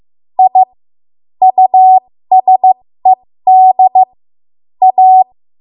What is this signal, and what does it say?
Morse code, heard as a keyed tone, about 15 wpm, spaced as for 13 wpm, 750 Hz. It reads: I USED A
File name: Morse / 13WPM